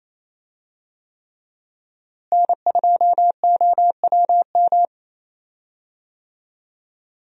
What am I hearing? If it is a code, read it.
Morse code, heard as a keyed tone, about 28 wpm, 700 Hz. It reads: N2OWM
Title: USA Call Signs - 5 Characters - Repeated 3x - 28wpm